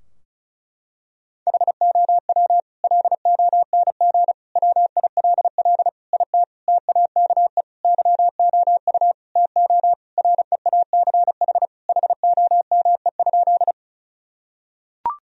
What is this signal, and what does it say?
Morse code, heard as a keyed tone, about 36 wpm, 700 Hz. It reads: HOW LONG WILL IT TAKE YOU TO REACH HOME?
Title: Sentences from Top 1000 Words - Review Part 2 - 35wpm